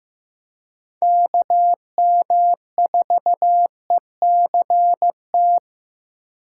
Morse code, heard as a keyed tone, about 15 wpm, 700 Hz. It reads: KM4ECT
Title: USA Call Signs - 6 Characters - 4x Character Spacing - 15wpm